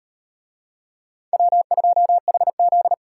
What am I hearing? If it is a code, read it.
Morse code, heard as a keyed tone, about 35 wpm, 700 Hz. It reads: W2HZ